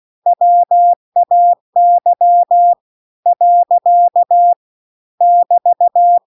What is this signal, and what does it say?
Morse code, heard as a keyed tone, about 16 wpm, 700 Hz. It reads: WAY . <BT>